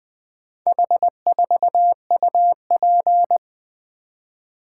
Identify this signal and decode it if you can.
Morse code, heard as a keyed tone, about 20 wpm, 700 Hz. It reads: H4UP
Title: Four Letter-Number Combinations - 20wpm